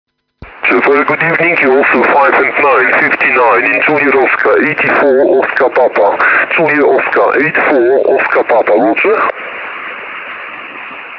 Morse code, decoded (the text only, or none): none